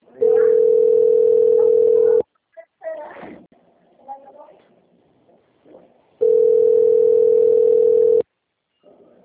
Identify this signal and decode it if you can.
Ringback tone (North America), 440 + 480 Hz, with speech.
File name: out- - - - - .68251